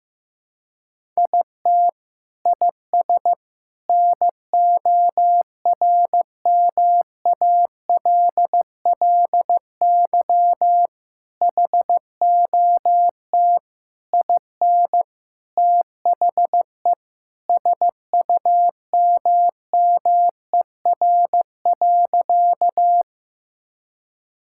Morse code, heard as a keyed tone, about 15 wpm, 700 Hz. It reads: IT IS NORMALLY HOT IN THE SUMMER.
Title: Sentences from Top 1000 Words - Review Part 6 - 15wpm